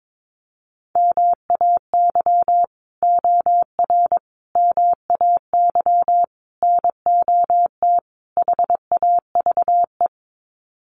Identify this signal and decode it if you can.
Morse code, heard as a keyed tone, about 22 wpm, 700 Hz. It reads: MAY OR MAY NOT HAVE
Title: Sets of 5 Words Encore2 22wpm